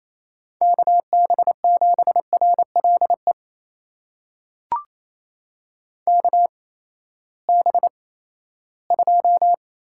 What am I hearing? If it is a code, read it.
Morse code, heard as a keyed tone, about 28 wpm, spaced as for 16 wpm, 700 Hz. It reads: KB7RLE KB2
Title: USA Call Signs - 6 Characters - 8x Character Spacing - 28wpm